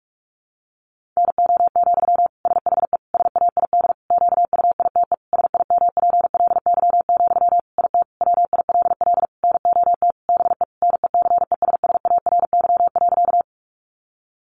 Morse code, heard as a keyed tone, about 44 wpm, 700 Hz. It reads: NO, SHE SAID QUITE SIMPLY, IT WILL NOT BE NECESSARY.
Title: Sentences from Top 800 Words 45wpm